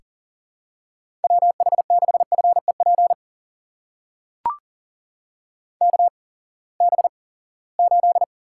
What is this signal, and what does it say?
Morse code, heard as a keyed tone, about 42 wpm, spaced as for 23 wpm, 700 Hz. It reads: WH6FEP KB8
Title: USA Call Signs - 6 Characters - 8x Character Spacing - 40wpm